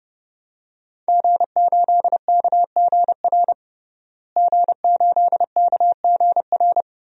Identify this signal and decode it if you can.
Morse code, heard as a keyed tone, about 30 wpm, 700 Hz. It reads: G8KGR G8KGR
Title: DXCC-100 Call Signs - 5 Characters - Repeated 2x - 30wpm